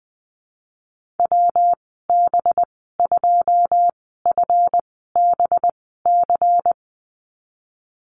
Morse code, heard as a keyed tone, about 20 wpm, spaced as for 15 wpm, 700 Hz. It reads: WB2FBC
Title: 213 Lesson 34g - Number 8 - Call Signs 20wpm